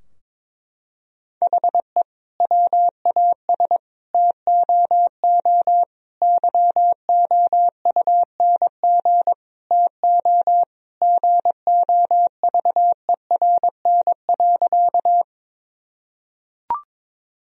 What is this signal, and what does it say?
Morse code, heard as a keyed tone, about 22 wpm, 700 Hz. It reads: HE WAS TOO YOUNG TO GOVERN.
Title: Sentences from Top 1000 Words - Review Part 2 - 22wpm